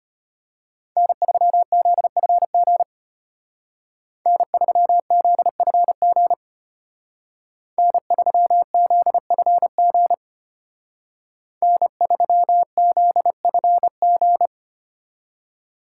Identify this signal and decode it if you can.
Morse code, heard as a keyed tone, about 30 wpm, 700 Hz. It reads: N3ZFG N3ZFG N3ZFG N3ZFG
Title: Speed Racing - USA 1x3 Call Signs 25wpm